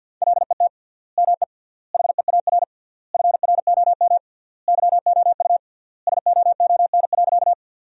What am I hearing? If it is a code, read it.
Morse code, heard as a keyed tone, about 52 wpm, 700 Hz. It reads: LET ME HEAR FROM YOU SOON.
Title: 3x Word-Spacing - Sentences from Top 400 Words - 50wpm